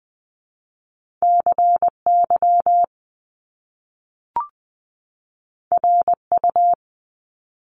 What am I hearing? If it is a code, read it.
Morse code, heard as a keyed tone, about 20 wpm, 700 Hz. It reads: CY RU
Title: Top Two Letter Combinations in Words 20wpm